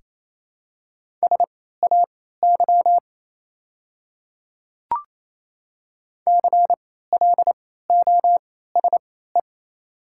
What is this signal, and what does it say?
Morse code, heard as a keyed tone, about 28 wpm, spaced as for 16 wpm, 700 Hz. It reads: SAY CLOSE
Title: 3x Character Spacing - Top 300 Words - 28wpm